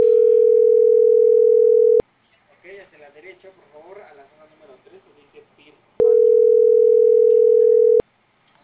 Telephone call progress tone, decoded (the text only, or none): ringback tone